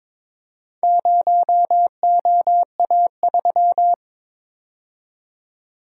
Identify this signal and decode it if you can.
Morse code, heard as a keyed tone, about 23 wpm, 700 Hz. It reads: 0OA3
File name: practice / 22wpm / Four Letter-Number Combinations - 22wpm